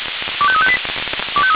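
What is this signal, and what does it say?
Five-tone selective call (ZVEI), heard as quick runs of five tones, tones of about 70 ms. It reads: 25358
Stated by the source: ZVEI five-tone call 25303 received by SDR.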